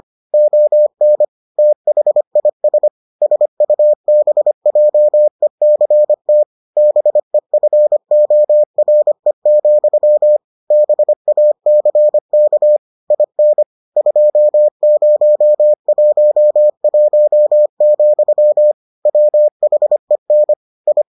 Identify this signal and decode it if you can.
Morse code, heard as a keyed tone, about 25 wpm, 600 Hz. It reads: ON THIS SUBJECT BEFORE, BACK IN 2011, WHEN I